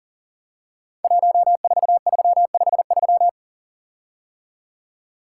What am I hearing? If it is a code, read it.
Morse code, heard as a keyed tone, about 39 wpm, 700 Hz. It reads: 14353
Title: Code Groups - 5 Letter:Numbers - 40wpm